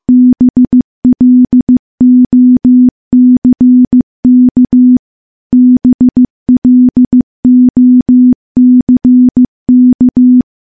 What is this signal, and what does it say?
Morse code, heard as a keyed tone, about 15 wpm, 250 Hz. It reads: BLOCK BLOCK
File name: morse / morse-training-15WPM